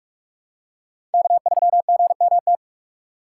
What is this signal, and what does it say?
Morse code, heard as a keyed tone, about 42 wpm, 700 Hz. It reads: K3GMT
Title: K1USN SST QSO Practice - 45wpm